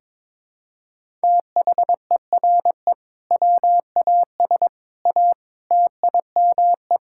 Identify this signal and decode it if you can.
Morse code, heard as a keyed tone, about 22 wpm, 700 Hz. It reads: THERE WAS A TIME
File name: practice / 22wpm / Build A Phrase - 5-Word Phrases - 22wpm